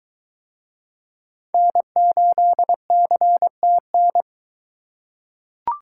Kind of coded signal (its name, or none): Morse code